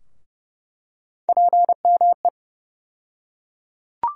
Morse code, heard as a keyed tone, about 32 wpm, 700 Hz. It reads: PME
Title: USA Call Sign Postfixes - 3 Letters 30wpm